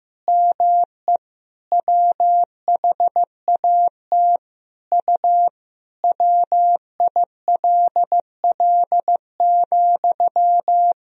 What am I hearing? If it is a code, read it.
Morse code, heard as a keyed tone, about 15 wpm, 700 Hz. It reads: ME WHAT U WILL,
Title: Sentences from Top 200 Words and Abbreviations 15wpm